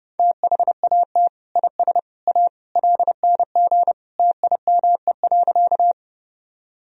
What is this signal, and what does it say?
Morse code, heard as a keyed tone, about 30 wpm, 700 Hz. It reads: THAT IS A LNG TIME.
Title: Sentences from Top 100 Words and Abbreviations 30wpm